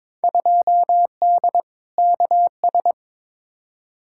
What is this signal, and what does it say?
Morse code, heard as a keyed tone, about 21 wpm, 700 Hz. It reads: 2D KS
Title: Field Day Exchanges Encore - 22wpm